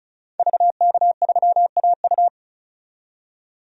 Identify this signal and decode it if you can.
Morse code, heard as a keyed tone, about 35 wpm, 700 Hz. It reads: VK3AU
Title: DXCC-100 Call Signs - 5 Characters - 35wpm